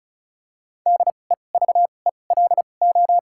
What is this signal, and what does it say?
Morse code, heard as a keyed tone, about 34 wpm, spaced as for 25 wpm, 700 Hz. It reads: DEVELO
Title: 2x Character Spacing - Top 500 Words - 35wpm